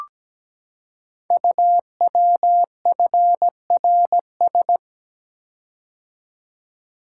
Morse code, heard as a keyed tone, about 17 wpm, 700 Hz. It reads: UWFRS
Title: Code Groups - 5 Letters - 17wpm